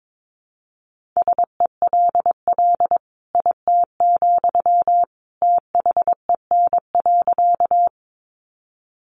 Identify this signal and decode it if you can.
Morse code, heard as a keyed tone, about 22 wpm, 700 Hz. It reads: SELL IT, THEN.
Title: Sentences from Top 1100 Words 22wpm